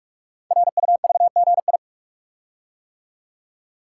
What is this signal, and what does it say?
Morse code, heard as a keyed tone, about 44 wpm, 700 Hz. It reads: RUVGI
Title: Code Groups - 5 Letter:Numbers - 45wpm